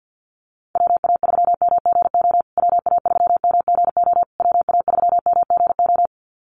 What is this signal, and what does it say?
Morse code, heard as a keyed tone, about 48 wpm, 700 Hz. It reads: WA3MGO WA3MGO WA3MGO
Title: USA Call Signs Repeated 3x 50wpm